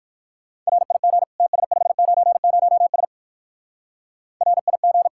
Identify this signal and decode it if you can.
Morse code, heard as a keyed tone, about 59 wpm, 700 Hz. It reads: RIG TS590S RIG